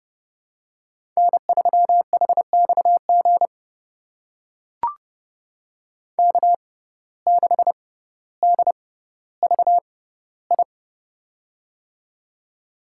Morse code, heard as a keyed tone, about 30 wpm, spaced as for 15 wpm, 700 Hz. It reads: N3HXG K6DVI
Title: USA Call Signs - 6x Character Spacing - 30wpm